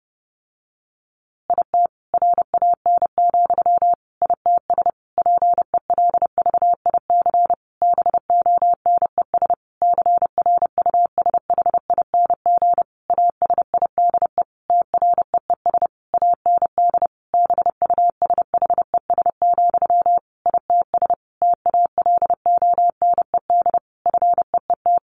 Morse code, heard as a keyed tone, about 30 wpm, 700 Hz. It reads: IT RAN, ITS PELVIC BONES CRUSHING ASIDE TREES AND BUSHES, ITS TALONED FEET
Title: A Sound Of Thunder 30wpm